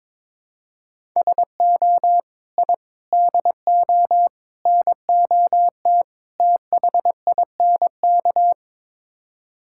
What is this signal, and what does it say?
Morse code, heard as a keyed tone, about 22 wpm, 700 Hz. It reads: SO I DO NOT THINK
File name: practice / 22wpm / Build A Phrase - 5-Word Phrases - 22wpm